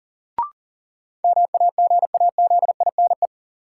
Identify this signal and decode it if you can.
Morse code, heard as a keyed tone, about 40 wpm, 700 Hz. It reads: MAGAZINE
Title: Sentences from Top 1000 Words - Review Part 7 - New Word Drill - 40wpm